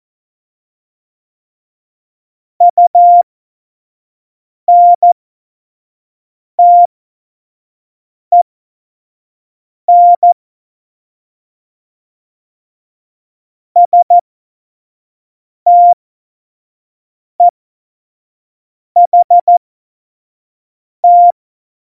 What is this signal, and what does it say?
Morse code, heard as a keyed tone, about 14 wpm, spaced as for 5 wpm, 700 Hz. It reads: UNTEN STEHT
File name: Morsekurs / Witze / Witz 24. November 2020, 25 F70